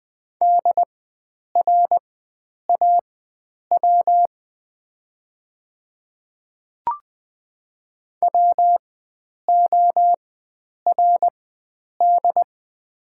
Morse code, heard as a keyed tone, about 20 wpm, spaced as for 9 wpm, 700 Hz. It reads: DRAW WORD
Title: 4x Character Spacing - Top 500 Words - 20wpm